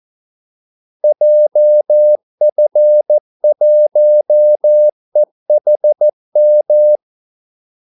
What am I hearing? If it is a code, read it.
Morse code, heard as a keyed tone, about 14 wpm, 600 Hz. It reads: JF1EHM
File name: Callsigns-14wpm